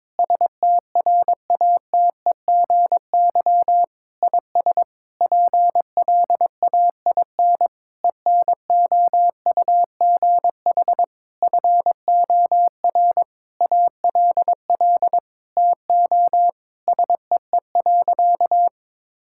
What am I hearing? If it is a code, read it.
Morse code, heard as a keyed tone, about 21 wpm, 700 Hz. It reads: STRATEGY IS PLAIN ENOUGH FOR ALL TO SEE.
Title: Sentences from Top 1000 Words - Review Part 5 - 22wpm